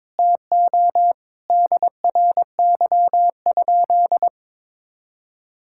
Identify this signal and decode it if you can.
Morse code, heard as a keyed tone, about 22 wpm, 700 Hz. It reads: TO DRY?
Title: Sentences from Top 1600 Words 22wpm